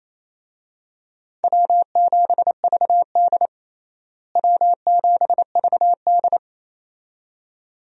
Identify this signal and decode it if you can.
Morse code, heard as a keyed tone, about 28 wpm, 700 Hz. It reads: W7VD W7VD
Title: USA Call Signs - 4 Characters - Repeated 2x - 28wpm